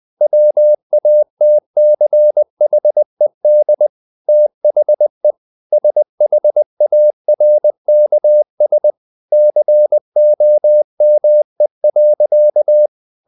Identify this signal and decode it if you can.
Morse code, heard as a keyed tone, about 20 wpm, 600 Hz. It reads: WATCHED THE SHARKS COME.
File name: the old man and the sea part 2 20wpm